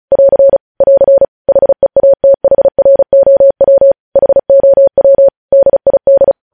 Morse code, heard as a keyed tone, about 36 wpm, 550 Hz. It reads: <AR> <AR> HEATHROW HOW DID
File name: podcast / News headlines-1x-35x35